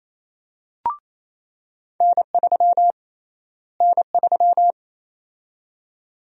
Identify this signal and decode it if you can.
Morse code, heard as a keyed tone, about 28 wpm, 700 Hz. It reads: N3 N3
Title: USA Call Sign Prefixes - Repeated 2x - 28wpm